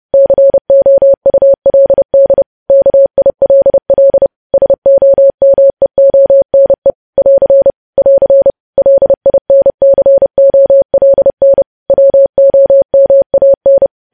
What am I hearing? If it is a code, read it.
Morse code, heard as a keyed tone, about 30 wpm, 550 Hz. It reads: COULD KILL SOMEONE <AR> <AR> LINCOLN WOMAN
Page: news headlines in morse code at 30 wpm